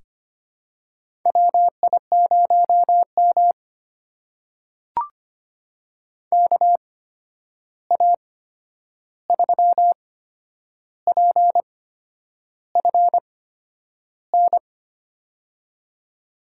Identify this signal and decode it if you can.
Morse code, heard as a keyed tone, about 25 wpm, spaced as for 9 wpm, 700 Hz. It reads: WI0M KA3PFN
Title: USA Call Signs - 8x Character Spacing - 25wpm